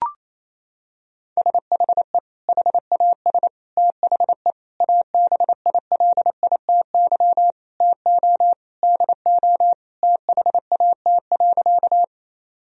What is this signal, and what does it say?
Morse code, heard as a keyed tone, about 28 wpm, 700 Hz. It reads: SHE HAS THE ABILITY TO DO THAT.
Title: Sentences from Top 1000 Words - Review Part 3 - 28wpm